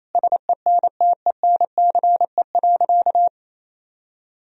Morse code, heard as a keyed tone, about 28 wpm, 700 Hz. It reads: SENTENCE.
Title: Sentences from Top 1000 Words - Review Part 2 - 28wpm